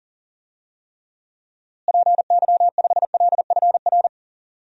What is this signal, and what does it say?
Morse code, heard as a keyed tone, about 37 wpm, 700 Hz. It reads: PY5LFR
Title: DXCC-100 Call Signs - 6 Characters - Repeated 2x - 40wpm